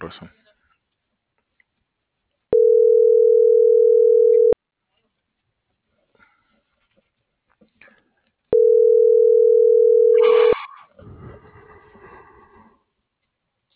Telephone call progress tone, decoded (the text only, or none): ringback tone